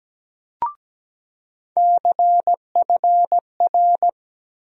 Morse code, heard as a keyed tone, about 17 wpm, 700 Hz. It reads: CFR
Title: USA Call Sign Postfixes - 3 Letters 17wpm